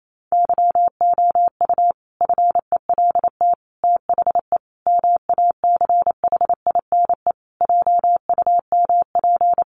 Morse code, heard as a keyed tone, about 28 wpm, 700 Hz. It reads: YOU FELT THE MACHINE JUMP
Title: A Sound Of Thunder 28wpm